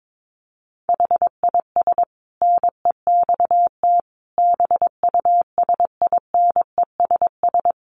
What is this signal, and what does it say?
Morse code, heard as a keyed tone, about 22 wpm, 700 Hz. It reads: HIS NEXT BUSINESS
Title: Sentences from Top 1300 Words 22wpm